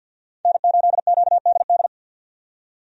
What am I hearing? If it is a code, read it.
Morse code, heard as a keyed tone, about 49 wpm, 700 Hz. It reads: N8QDD
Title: USA Call Signs - 5 Characters - 50wpm